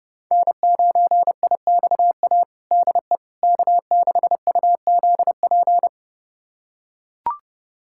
Morse code, heard as a keyed tone, about 32 wpm, 700 Hz. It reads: N9IXA DE K6UZP
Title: USA Callsign DE Callsign - 5 Characters - 30wpm